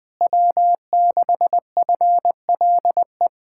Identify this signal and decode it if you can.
Morse code, heard as a keyed tone, about 20 wpm, 700 Hz. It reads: W6FLE
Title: Speed-Racing Common QSO Phrases - 20wpm